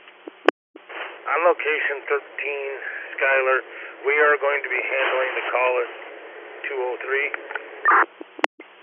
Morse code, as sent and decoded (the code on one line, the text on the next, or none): none